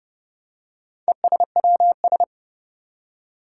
Morse code, heard as a keyed tone, about 30 wpm, 700 Hz. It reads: ESWS
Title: Four Challenging Letter-Number Combinations 30wpm